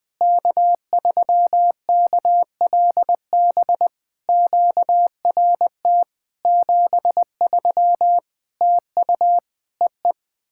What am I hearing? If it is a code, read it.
Morse code, heard as a keyed tone, about 21 wpm, 700 Hz. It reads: K3KLB QRT 73 TU EE
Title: POTA QSO Practice - 20wpm